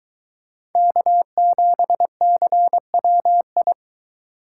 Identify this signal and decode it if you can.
Morse code, heard as a keyed tone, about 23 wpm, 700 Hz. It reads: K7CWI